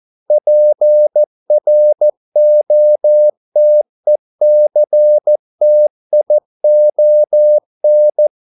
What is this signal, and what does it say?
Morse code, heard as a keyed tone, about 14 wpm, 600 Hz. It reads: PROTECTION